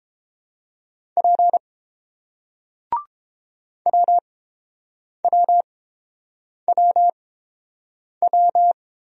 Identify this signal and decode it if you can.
Morse code, heard as a keyed tone, about 30 wpm, spaced as for 7 wpm, 700 Hz. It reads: P WWWW